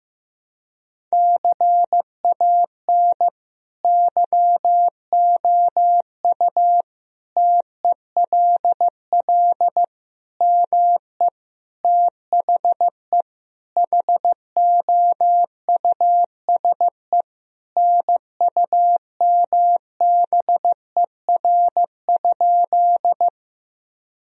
Morse code, heard as a keyed tone, about 15 wpm, 700 Hz. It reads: CAN YOU TELL ME THE HOUSE NUMBER?
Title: Sentences from Top 1000 Words - Review Part 1 - 15wpm